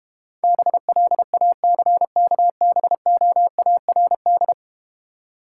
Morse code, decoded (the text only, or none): BLACKBOARD